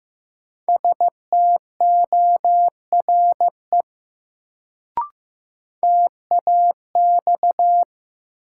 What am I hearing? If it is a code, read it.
Morse code, heard as a keyed tone, about 15 wpm, 700 Hz. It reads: STORE TAX
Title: Sentences from Top 1000 Words - Review Part 5 - New Word Drill - 15wpm